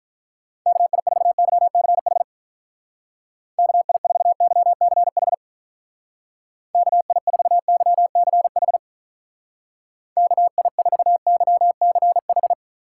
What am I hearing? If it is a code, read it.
Morse code, heard as a keyed tone, about 44 wpm, 700 Hz. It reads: KI4YCH KI4YCH KI4YCH KI4YCH